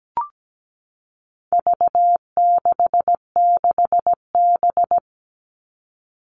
Morse code, heard as a keyed tone, about 17 wpm, 700 Hz. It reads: V66B
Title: All Cut Numbers - 4-Digit 17wpm